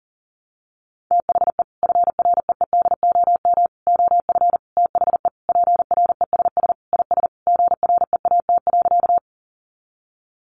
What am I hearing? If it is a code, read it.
Morse code, heard as a keyed tone, about 37 wpm, 700 Hz. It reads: THE FREEDOM OF THE PRESS IS GREAT.
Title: Sentences from Top 800 Words 40wpm